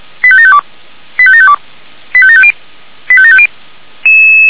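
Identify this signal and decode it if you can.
Five-tone selective call (ZVEI), heard as quick runs of five tones, tones of about 80 ms. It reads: 85752 85752 85750 85750
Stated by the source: ZVEI five-tone call received by SDR.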